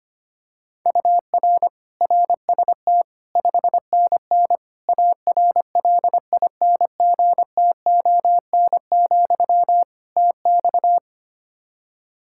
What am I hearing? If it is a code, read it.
Morse code, heard as a keyed tone, about 25 wpm, 700 Hz. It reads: UR RST 5NN ARLINGTON, TX
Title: Practice QSO - Exchange 1 - Beginning - 25wpm